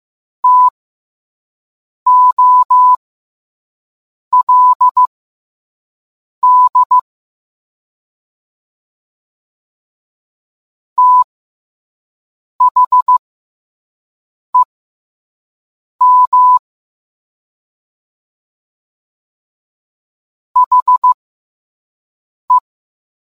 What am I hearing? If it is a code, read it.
Morse code, heard as a keyed tone, about 15 wpm, spaced as for 5 wpm, 1000 Hz. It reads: TOLD THEM HE